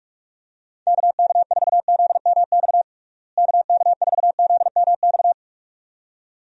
Morse code, heard as a keyed tone, about 46 wpm, 700 Hz. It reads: KK4ZMX KK4ZMX
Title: USA Call Signs - 6 Characters - Repeated 2x - 45wpm